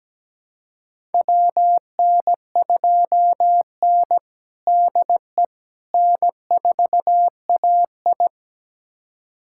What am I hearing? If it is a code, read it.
Morse code, heard as a keyed tone, about 17 wpm, 700 Hz. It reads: WN2N DE N4AI
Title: USA Callsign DE Callsign - 4 Characters - 17wpm